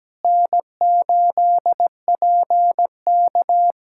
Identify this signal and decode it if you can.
Morse code, heard as a keyed tone, about 17 wpm, 700 Hz. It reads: N8PK
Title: USA Callsign DE Callsign - 4 Characters - 17wpm